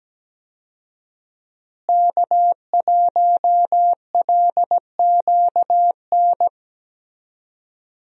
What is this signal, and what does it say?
Morse code, heard as a keyed tone, about 17 wpm, 700 Hz. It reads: K1LQN
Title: USA Call Signs - 5 Characters - Repeated 2x - 17wpm